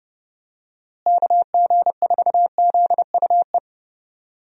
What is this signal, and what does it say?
Morse code, heard as a keyed tone, about 29 wpm, 700 Hz. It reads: KG4ZUE
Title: USA Call Signs - 6 Characters - Repeated 2x - 30wpm